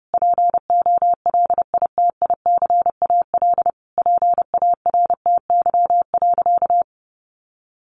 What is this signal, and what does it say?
Morse code, heard as a keyed tone, about 30 wpm, 700 Hz. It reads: POLITICAL PARTY.
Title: Sentences from Top 1400 Words 30wpm